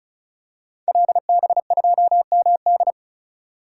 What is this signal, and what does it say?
Morse code, heard as a keyed tone, about 34 wpm, 700 Hz. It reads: LB2MD
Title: Code Groups - 5 Letter:Numbers - 35wpm